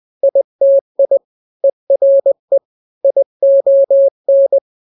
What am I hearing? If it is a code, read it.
Morse code, heard as a keyed tone, about 20 wpm, 550 Hz. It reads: ITI ERE ION
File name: n-grams / 3-grams 20 18 WPM